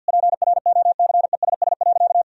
Morse code, heard as a keyed tone, about 50 wpm, 700 Hz. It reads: PROCESS.
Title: Sentences from Top 1000 Words - Review Part 3 - 50wpm